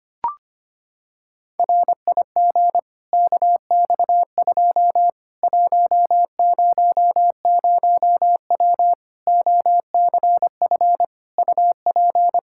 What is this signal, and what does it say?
Morse code, heard as a keyed tone, about 25 wpm, 700 Hz. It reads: RIG KX2 100W OCF UP
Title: Practice QSO - Exchange 2 - Info - 25wpm